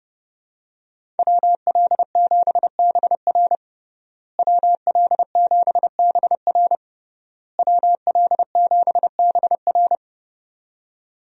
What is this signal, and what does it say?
Morse code, heard as a keyed tone, about 30 wpm, 700 Hz. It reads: WL7BR WL7BR WL7BR
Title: USA Call Signs - 5 Characters - Repeated 3x - 30wpm